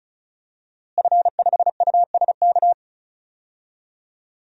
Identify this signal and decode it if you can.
Morse code, heard as a keyed tone, about 34 wpm, 700 Hz. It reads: F5USK
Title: DXCC-100 Call Signs - 5 Characters - Repeated 2x - 35wpm